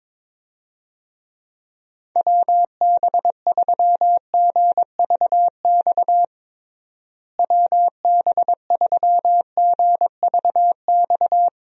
Morse code, heard as a keyed tone, about 22 wpm, 700 Hz. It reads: WB3GVX WB3GVX
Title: USA Call Signs - 6 Characters - Repeated 2x - 22wpm